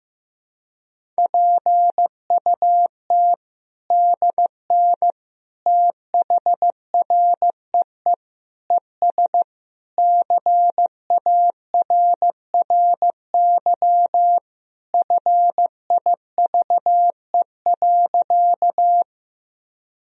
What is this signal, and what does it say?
Morse code, heard as a keyed tone, about 15 wpm, 700 Hz. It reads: PUT DN THREE ES CARRY FIVE.
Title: Sentences from Top 400 Words and Abbreviations 15wpm